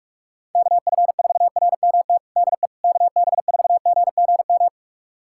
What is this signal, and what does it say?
Morse code, heard as a keyed tone, about 46 wpm, 700 Hz. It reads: KF4RMT DE KB4GGM
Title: USA Callsign DE Callsign - 6 Characters - 45wpm